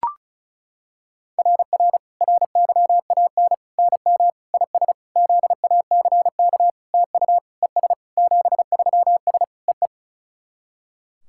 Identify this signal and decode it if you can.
Morse code, heard as a keyed tone, about 36 wpm, 690 Hz. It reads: RR RYAN NM IS ZACK TU ES 73S EE
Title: SOTA QSO Practice 35wpm